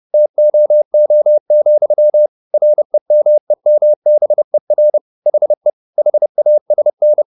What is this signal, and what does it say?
Morse code, heard as a keyed tone, about 30 wpm, 600 Hz. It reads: TOO, REMEMBER HE HASN